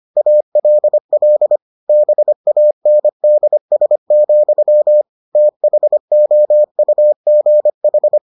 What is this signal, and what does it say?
Morse code, heard as a keyed tone, about 25 wpm, 600 Hz. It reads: ALL BANDS, THOUGH